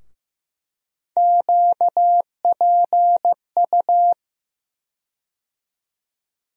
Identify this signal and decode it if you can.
Morse code, heard as a keyed tone, about 15 wpm, 700 Hz. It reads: QPU
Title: USA Call Sign Postfixes - 3 Letters 15wpm